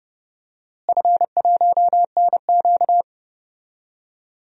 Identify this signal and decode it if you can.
Morse code, heard as a keyed tone, about 30 wpm, 700 Hz. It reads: F1NQ
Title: DXCC-100 Call Signs - 4 Characters - Repeated 2x - 30wpm